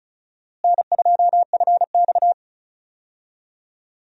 Morse code, heard as a keyed tone, about 34 wpm, 700 Hz. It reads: N2FX
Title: USA Call Signs - 4 Characters - Repeated 2x - 35wpm